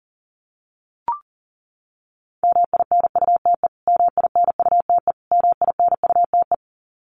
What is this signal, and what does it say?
Morse code, heard as a keyed tone, about 40 wpm, 700 Hz. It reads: MINUTE MINUTE MINUTE
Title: Top 500 Words Repeated 40wpm